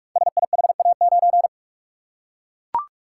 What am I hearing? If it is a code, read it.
Morse code, heard as a keyed tone, about 44 wpm, 700 Hz. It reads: SIHA9
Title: Code Groups - 5 Letter:Numbers - 45wpm